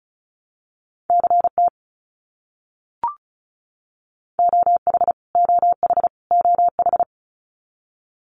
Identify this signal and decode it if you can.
Morse code, heard as a keyed tone, about 34 wpm, 700 Hz. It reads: CT OH OH OH
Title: US States Repeated 3x 35wpm